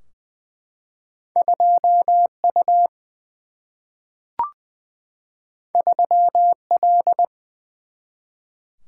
Morse code, heard as a keyed tone, about 20 wpm, 700 Hz. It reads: 2 U 3 L
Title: Single-Letter Single-Number 20wpm